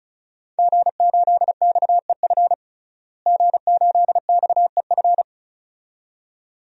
Morse code, heard as a keyed tone, about 36 wpm, 700 Hz. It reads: G8XEF G8XEF
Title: DXCC-100 Call Signs - 5 Characters - Repeated 2x - 35wpm